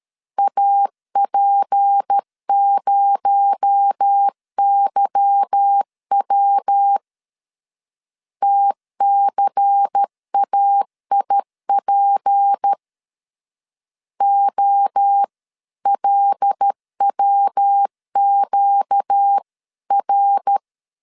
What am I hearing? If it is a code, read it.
Morse code, heard as a keyed tone, about 13 wpm, 800 Hz. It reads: AP0YW TCAIP OLWQR
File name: CW / DET / 1 9 1500C S A 12wpm 3800